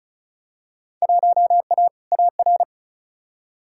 Morse code, heard as a keyed tone, about 36 wpm, 700 Hz. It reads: 1A AR
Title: Field Day Exchanges - 35wpm